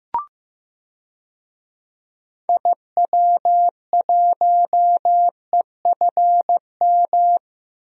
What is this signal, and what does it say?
Morse code, heard as a keyed tone, about 15 wpm, 700 Hz. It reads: IW1EFM
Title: DXCC-100 Call Signs - 6 Characters - Repeated 2x - 15wpm